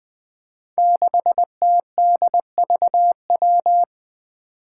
Morse code, heard as a keyed tone, about 20 wpm, 700 Hz. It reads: 6TDVW
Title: Five Challenging Letter-Number Combinations 20wpm